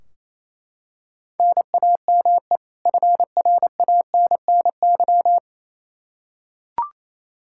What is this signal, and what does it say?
Morse code, heard as a keyed tone, about 28 wpm, 700 Hz. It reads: NAME FRANNY
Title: Practice QSO - Exchange 1 - Beginning - 28wpm